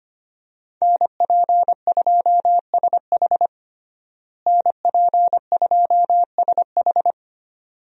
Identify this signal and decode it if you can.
Morse code, heard as a keyed tone, about 25 wpm, 700 Hz. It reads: NP2SH NP2SH
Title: USA Call Signs - 5 Characters - Repeated 2x - 25wpm